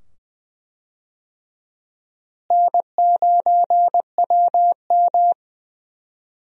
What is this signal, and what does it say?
Morse code, heard as a keyed tone, about 20 wpm, 700 Hz. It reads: N9WM
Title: USA Call Signs - 4 Characters - Repeated 2x - 20wpm